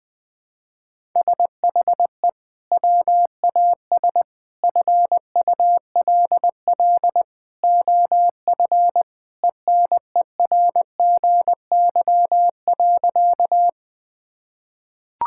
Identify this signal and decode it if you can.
Morse code, heard as a keyed tone, about 20 wpm, 700 Hz. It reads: SHE WAS FULL OF ENERGY.
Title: Sentences from Top 1000 Words - Review Part 6 - 20wpm